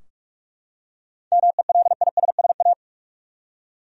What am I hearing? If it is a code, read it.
Morse code, heard as a keyed tone, about 43 wpm, 700 Hz. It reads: MELISSA
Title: USA Common Names - 45wpm